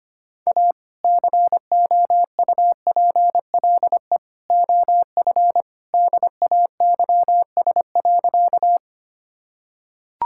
Morse code, heard as a keyed tone, about 25 wpm, 700 Hz. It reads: A COUPLE OF DAYS.
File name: practice / 25wpm / Sentences with 2-Word Phrases - 25wpm